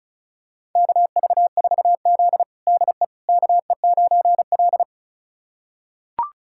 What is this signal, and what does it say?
Morse code, heard as a keyed tone, about 34 wpm, 700 Hz. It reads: KV4Z DE KE9L
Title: USA Callsign DE Callsign - 4 Characters - 35wpm